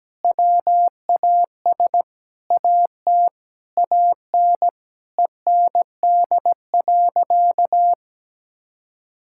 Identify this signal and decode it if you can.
Morse code, heard as a keyed tone, about 17 wpm, 700 Hz. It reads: WAS AT AN END.
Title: Sentences from Top 1000 Words - Review Part 6 - 17wpm